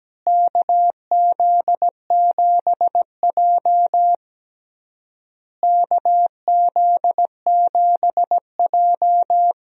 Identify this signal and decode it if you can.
Morse code, heard as a keyed tone, about 17 wpm, 700 Hz. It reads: KZ7J KZ7J
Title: USA Call Signs - 4 Characters - Repeated 3x - 17wpm